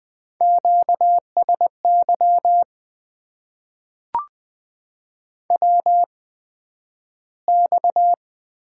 Morse code, heard as a keyed tone, about 20 wpm, spaced as for 11 wpm, 700 Hz. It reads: QSY WX
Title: Top 90 QSO Elements - 8x Character Spacing - 20wpm